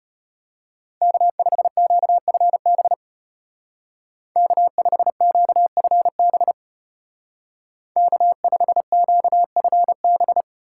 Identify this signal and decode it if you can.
Morse code, heard as a keyed tone, about 31 wpm, 700 Hz. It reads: K5QFB K5QFB K5QFB
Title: Speed Racing - USA Call Signs 25wpm